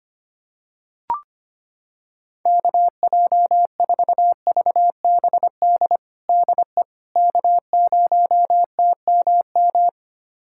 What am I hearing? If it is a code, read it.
Morse code, heard as a keyed tone, about 25 wpm, 700 Hz. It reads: KJ4VBD DE K0TMM
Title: USA Callsign DE Callsign 25wpm